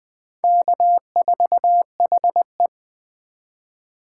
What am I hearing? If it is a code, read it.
Morse code, heard as a keyed tone, about 20 wpm, 700 Hz. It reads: K4HE